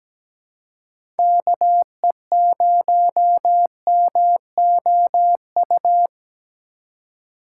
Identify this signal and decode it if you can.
Morse code, heard as a keyed tone, about 17 wpm, 700 Hz. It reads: KE0MOU
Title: USA Call Signs - 6 Characters - 2x Character Spacing - 17wpm